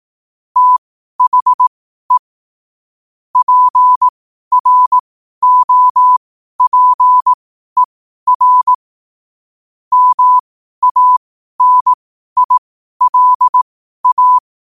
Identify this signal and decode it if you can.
Morse code, heard as a keyed tone, about 18 wpm, spaced as for 12 wpm, 1000 Hz. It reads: THE PROPER MANILA
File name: qotdfiles / 12wpm